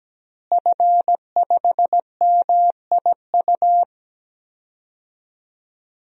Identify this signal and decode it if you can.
Morse code, heard as a keyed tone, about 17 wpm, 700 Hz. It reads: F5MIU
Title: DXCC-100 Call Signs - 5 Characters - Repeated 2x - 17wpm